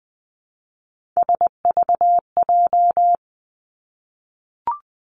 Morse code, heard as a keyed tone, about 20 wpm, 700 Hz. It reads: SVJ
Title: Three Challenging Letter-Number Combinations 20wpm